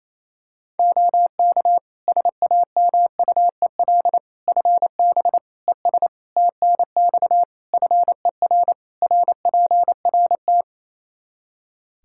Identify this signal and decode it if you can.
Morse code, heard as a keyed tone, about 28 wpm, 700 Hz. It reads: OK SAMUEL FB ES TNX FER RPRT
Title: K1USN SST QSO Practice 28wpm